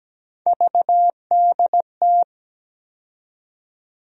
Morse code, heard as a keyed tone, about 17 wpm, 700 Hz. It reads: VDT
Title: USA Call Sign Postfixes - 3 Letters 17wpm